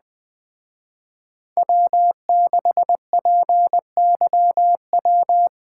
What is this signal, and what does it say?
Morse code, heard as a keyed tone, about 21 wpm, 700 Hz. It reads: W6PYW
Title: Speed Racing - USA 1x3 Call Signs 17wpm